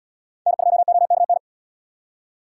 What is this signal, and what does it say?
Morse code, heard as a keyed tone, about 77 wpm, 700 Hz. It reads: N2OKN